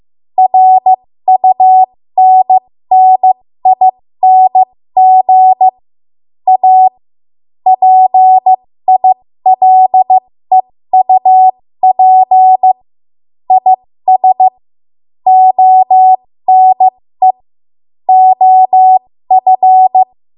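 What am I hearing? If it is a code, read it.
Morse code, heard as a keyed tone, about 15 wpm, spaced as for 13 wpm, 750 Hz. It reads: RUNNING A PILEUP IS ONE OF
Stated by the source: 13 wpm Morse code practice archive transmission.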